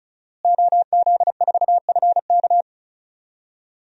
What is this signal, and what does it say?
Morse code, heard as a keyed tone, about 34 wpm, 700 Hz. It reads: OZ4FK
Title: DXCC-100 Call Signs - 5 Characters - Repeated 2x - 35wpm